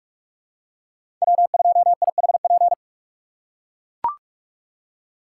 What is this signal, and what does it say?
Morse code, heard as a keyed tone, about 43 wpm, 700 Hz. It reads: W2IHP
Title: USA Call Signs - 5 Characters - 2x Character Spacing - 45wpm